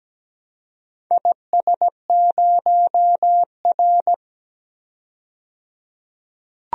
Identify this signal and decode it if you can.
Morse code, heard as a keyed tone, about 17 wpm, 700 Hz. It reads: IS0R
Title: DXCC-100 Call Signs - 4 Characters - 17wpm